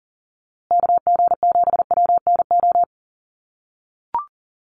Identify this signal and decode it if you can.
Morse code, heard as a keyed tone, about 40 wpm, 700 Hz. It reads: KG7WNO
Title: USA Call Signs Repeated 2x 40wpm